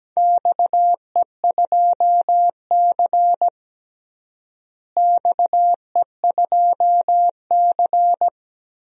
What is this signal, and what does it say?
Morse code, heard as a keyed tone, about 17 wpm, 700 Hz. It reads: XE2C XE2C
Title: DXCC-100 Call Signs - 4 Characters - Repeated 3x - 17wpm